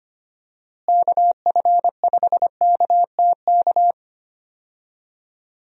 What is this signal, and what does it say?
Morse code, heard as a keyed tone, about 25 wpm, 700 Hz. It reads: KF5KTK
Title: USA Call Signs - 6 Characters - Repeated 3x - 25wpm